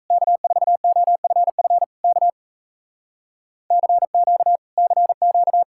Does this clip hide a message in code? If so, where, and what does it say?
Yes: Morse code from 0.1 to 5.6 s, heard as a keyed tone, about 39 wpm, 700 Hz. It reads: K4OFF K CQ CQ